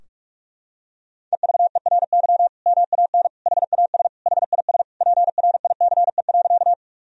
Morse code, heard as a keyed tone, about 46 wpm, 700 Hz. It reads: EVERY MAN HAS HIS PRICE.
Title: Sentences from Top 1000 Words - Review Part 4 - 45wpm